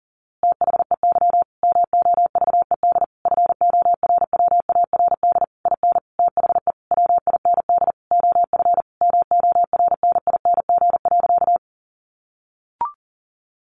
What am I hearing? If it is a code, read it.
Morse code, heard as a keyed tone, about 40 wpm, 700 Hz. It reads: THEY MOVED FORWARD IN THE WIND OF MORNING.
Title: A Sound Of Thunder 40wpm